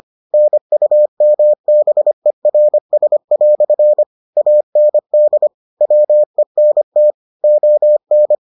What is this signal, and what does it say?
Morse code, heard as a keyed tone, about 25 wpm, 600 Hz. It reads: NUMBERS" AND WENT ON